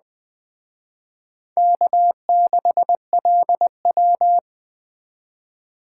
Morse code, heard as a keyed tone, about 20 wpm, 700 Hz. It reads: K6LW